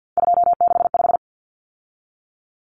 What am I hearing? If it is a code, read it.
Morse code, heard as a keyed tone, about 50 wpm, 700 Hz. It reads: 265
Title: Numbers - 3-Digit 50wpm